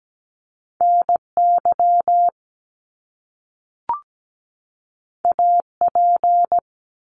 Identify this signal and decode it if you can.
Morse code, heard as a keyed tone, about 17 wpm, 700 Hz. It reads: NY AP
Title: Top Two Letter Combinations in Words 17wpm